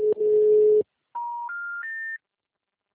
Special information tone.